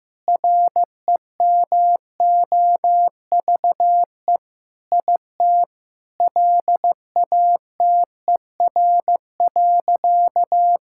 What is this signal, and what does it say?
Morse code, heard as a keyed tone, about 16 wpm, 700 Hz. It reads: REMOVE IT LATER.
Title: Sentences from Top 1000 Words - Review Part 5 - 15wpm